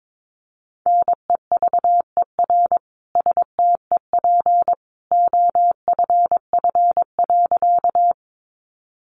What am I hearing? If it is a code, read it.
Morse code, heard as a keyed tone, about 22 wpm, 700 Hz. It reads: NEVER STEP OFF.
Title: A Sound Of Thunder 22wpm